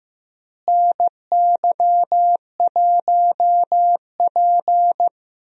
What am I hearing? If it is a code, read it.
Morse code, heard as a keyed tone, about 15 wpm, 700 Hz. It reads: NY1P
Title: USA Call Signs - 4 Characters - Repeated 2x - 15wpm